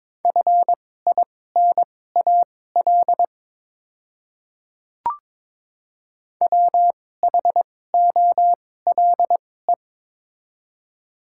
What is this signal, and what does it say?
Morse code, heard as a keyed tone, about 22 wpm, spaced as for 16 wpm, 700 Hz. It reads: FINAL WHOLE
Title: 2x Character Spacing - Top 500 Words - 22wpm